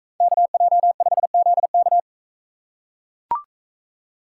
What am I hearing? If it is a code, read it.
Morse code, heard as a keyed tone, about 43 wpm, 700 Hz. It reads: KJ5ZK